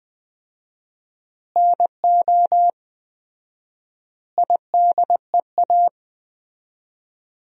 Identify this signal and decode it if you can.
Morse code, heard as a keyed tone, about 20 wpm, 700 Hz. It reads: NO IDEA
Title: 4x Word-Spacing - Sentences from Top 300 Words - 20wpm